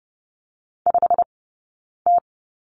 Morse code, heard as a keyed tone, about 30 wpm, spaced as for 9 wpm, 690 Hz. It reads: 5T
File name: Lesson 17d - Letter u - Two Characters 30wpm